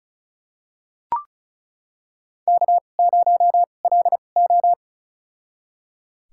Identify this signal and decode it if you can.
Morse code, heard as a keyed tone, about 36 wpm, spaced as for 25 wpm, 700 Hz. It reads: K0LO
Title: USA Call Signs - 4 Characters - 2x Character Spacing - 35wpm